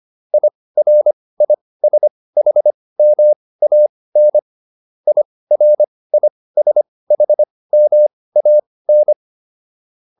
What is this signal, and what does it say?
Morse code, heard as a keyed tone, about 25 wpm, spaced as for 18 wpm, 600 Hz. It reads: IRISHMAN IRISHMAN